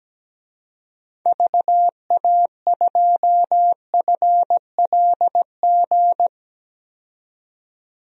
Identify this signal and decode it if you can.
Morse code, heard as a keyed tone, about 17 wpm, 700 Hz. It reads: VA2FLG
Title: DXCC-100 Call Signs - 6 Characters - 17wpm